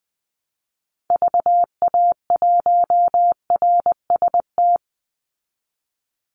Morse code, heard as a keyed tone, about 20 wpm, 700 Hz. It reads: VA1RST
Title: Call Signs Repeated 3x - 20wpm